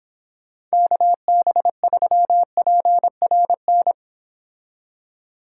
Morse code, heard as a keyed tone, about 26 wpm, 700 Hz. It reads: KB3PRN